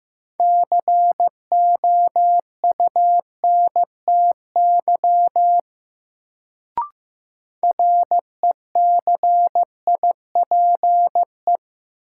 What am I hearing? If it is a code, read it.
Morse code, heard as a keyed tone, about 15 wpm, 700 Hz. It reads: COUNTY RECIPE
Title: Sentences from Top 1200 Words - New Word Drill - 15wpm